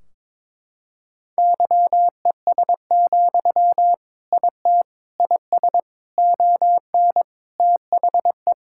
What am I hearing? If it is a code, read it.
Morse code, heard as a keyed tone, about 22 wpm, 700 Hz. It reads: YES, IT IS ON THE
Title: Sentences from Top 1600 Words 22wpm